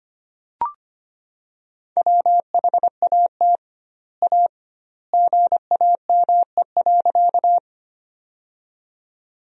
Morse code, heard as a keyed tone, about 25 wpm, 700 Hz. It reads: WHAT A GAME.
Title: 2x Word-Spacing - Sentences from Top 500 Words - 25wpm